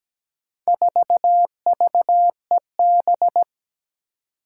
Morse code, heard as a keyed tone, about 17 wpm, 700 Hz. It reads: 4VEB
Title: Four Challenging Letter-Number Combinations 17wpm